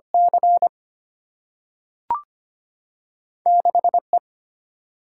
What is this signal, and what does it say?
Morse code, heard as a keyed tone, about 25 wpm, 700 Hz. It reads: C 6 E
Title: Single-Letter Single-Number 25wpm